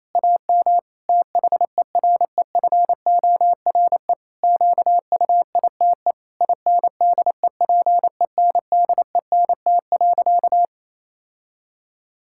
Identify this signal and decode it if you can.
Morse code, heard as a keyed tone, about 28 wpm, 700 Hz. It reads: AM THEREFORE QUITE INDEPENDENT.
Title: Sentences from Top 1000 Words - Review Part 6 - 28wpm